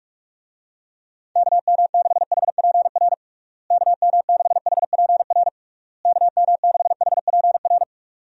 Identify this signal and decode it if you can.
Morse code, heard as a keyed tone, about 41 wpm, 700 Hz. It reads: KM6HPR KM6HPR KM6HPR
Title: USA Call Signs - 6 Characters - Repeated 3x - 45wpm